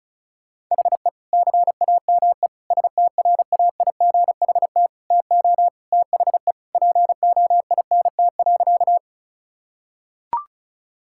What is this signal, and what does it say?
Morse code, heard as a keyed tone, about 34 wpm, 700 Hz. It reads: HE CAME STRAIGHT TO THE POINT.
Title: Sentences from Top 1000 Words - Review Part 6 - 35wpm